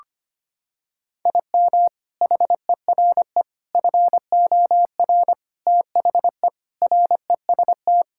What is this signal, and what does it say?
Morse code, heard as a keyed tone, about 25 wpm, 700 Hz. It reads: IM HERE FOR THE REST
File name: practice / 25wpm / Sentences with 4-Word Phrases - 25wpm